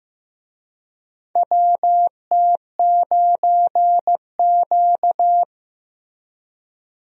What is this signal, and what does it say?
Morse code, heard as a keyed tone, about 15 wpm, 700 Hz. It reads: WT9Q
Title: USA Call Signs - 4 Characters - Repeated 3x - 15wpm